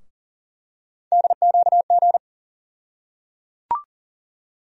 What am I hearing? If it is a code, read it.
Morse code, heard as a keyed tone, about 38 wpm, 700 Hz. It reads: DQG
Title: USA Call Sign Postfixes - 3 Letters 40wpm